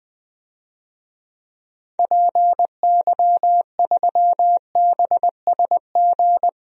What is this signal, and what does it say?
Morse code, heard as a keyed tone, about 20 wpm, 700 Hz. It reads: PY3BSG
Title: DXCC-100 Call Signs - 6 Characters - 20wpm